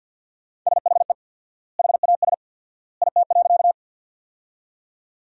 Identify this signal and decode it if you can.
Morse code, heard as a keyed tone, about 49 wpm, 700 Hz. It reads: SHE HAS IT.
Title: 4x Word-Spacing - Sentences from Top 100 Words - 50wpm